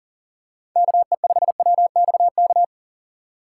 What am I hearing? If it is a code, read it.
Morse code, heard as a keyed tone, about 40 wpm, 700 Hz. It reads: KE5WXK
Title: USA Call Signs - 3x Character Spacing - 40wpm